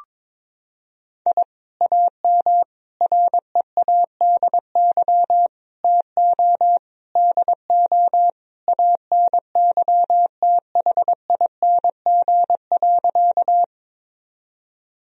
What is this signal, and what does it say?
Morse code, heard as a keyed tone, about 22 wpm, 700 Hz. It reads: I AM READY TO DO ANYTHING.
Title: Sentences from Top 1000 Words - Review Part 2 - 22wpm